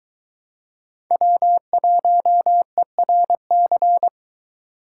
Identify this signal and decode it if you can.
Morse code, heard as a keyed tone, about 22 wpm, 700 Hz. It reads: W1ERC